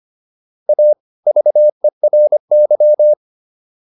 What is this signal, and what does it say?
Morse code, heard as a keyed tone, about 25 wpm, 600 Hz. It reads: A VERY